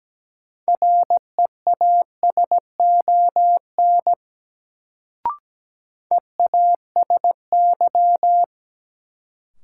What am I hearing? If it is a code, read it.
Morse code, heard as a keyed tone, about 17 wpm, 700 Hz. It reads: REASON EASY
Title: Sentences from Top 700 Words - New Word Drill - 17wpm